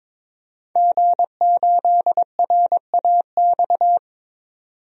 Morse code, heard as a keyed tone, about 21 wpm, 700 Hz. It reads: G8RAX
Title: DXCC-100 Call Signs - 5 Characters - 22wpm